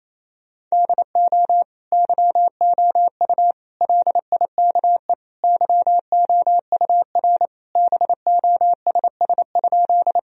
Morse code, heard as a keyed tone, about 28 wpm, 700 Hz. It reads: DO YOU LIKE YOUR BOSS?
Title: Sentences from Top 1000 Words - Review Part 4 - 28wpm